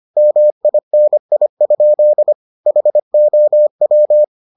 Morse code, heard as a keyed tone, about 25 wpm, 600 Hz. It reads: MINI? HOW